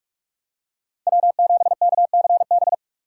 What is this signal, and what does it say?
Morse code, heard as a keyed tone, about 45 wpm, 700 Hz. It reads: W7KCB
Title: USA Call Signs - 5 Characters - 4x Character Spacing - 45wpm